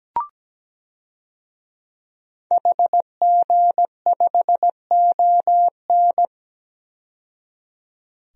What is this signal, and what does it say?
Morse code, heard as a keyed tone, about 17 wpm, 700 Hz. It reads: HG5ON
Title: DXCC-100 Call Signs - 5 Characters - 17wpm